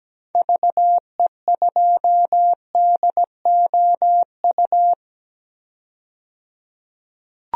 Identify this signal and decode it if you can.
Morse code, heard as a keyed tone, about 17 wpm, 700 Hz. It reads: VE2DOU
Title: DXCC-100 Call Signs - 6 Characters - 17wpm